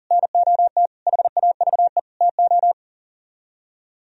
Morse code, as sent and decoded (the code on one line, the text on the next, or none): -. --- - / .... .- ...- . / - ---
NOT HAVE TO